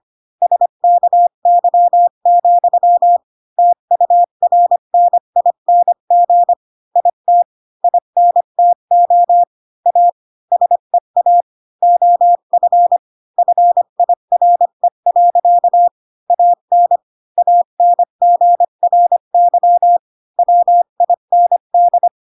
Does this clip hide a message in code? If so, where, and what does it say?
Yes: Morse code from 0.4 to 22.1 s, heard as a keyed tone, about 25 wpm, 700 Hz. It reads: SKY, TURNING IT INTO A SEA OF FIRE. AN ANGRY WIND